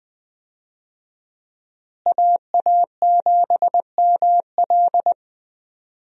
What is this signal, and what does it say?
Morse code, heard as a keyed tone, about 20 wpm, 700 Hz. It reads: AA7ML
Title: USA Call Signs - 5 Characters - Repeated 3x - 20wpm